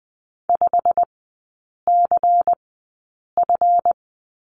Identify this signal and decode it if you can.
Morse code, heard as a keyed tone, about 20 wpm, spaced as for 8 wpm, 700 Hz. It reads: 5CF
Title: 184 Lesson 30e - Letter b - Three Characters 20wpm